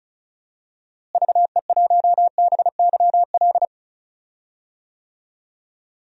Morse code, heard as a keyed tone, about 36 wpm, 700 Hz. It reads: VE1BYL
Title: DXCC-100 Call Signs - 6 Characters - Repeated 3x - 35wpm